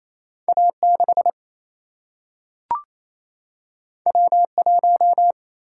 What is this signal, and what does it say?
Morse code, heard as a keyed tone, about 28 wpm, 700 Hz. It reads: A 6 W 1
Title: Single-Letter Single-Number 28wpm